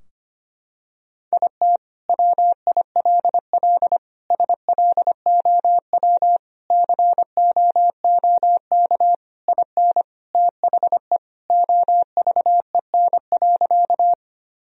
Morse code, heard as a keyed tone, about 25 wpm, 700 Hz. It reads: IT WILL SLOW COOK IN THE OVEN.
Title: Sentences from Top 1000 Words - Review Part 4 - 25wpm